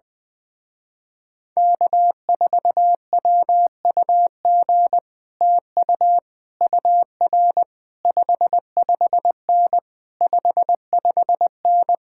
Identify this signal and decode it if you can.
Morse code, heard as a keyed tone, about 20 wpm, 700 Hz. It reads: K4WUG TU UR 55N 55N
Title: Speed-Racing Common QSO Phrases - 20wpm